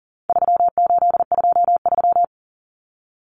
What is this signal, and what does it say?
Morse code, heard as a keyed tone, about 40 wpm, 700 Hz. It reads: 3823
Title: Common Cut Numbers - 4-Digit 40wpm